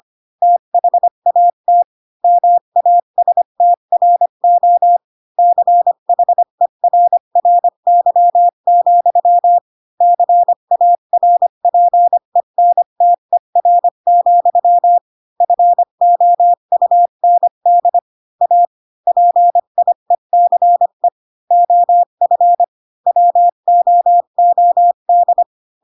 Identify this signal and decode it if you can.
Morse code, heard as a keyed tone, about 25 wpm, 700 Hz. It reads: THAT MASTRO CHERRY, CARPENTER, FOUND A PIECE OF WOOD
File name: Pinocchio / Pinocchio ch1 20 25 ik1hge